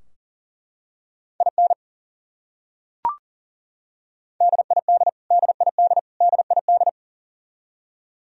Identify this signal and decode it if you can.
Morse code, heard as a keyed tone, about 40 wpm, 700 Hz. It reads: IN DID DID DID
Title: Top 100 Words Repeated 40wpm